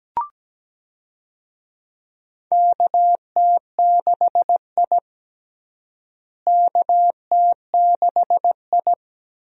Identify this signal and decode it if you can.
Morse code, heard as a keyed tone, about 17 wpm, 700 Hz. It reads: KT6I KT6I
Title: USA Call Signs - 4 Characters - Repeated 3x - 17wpm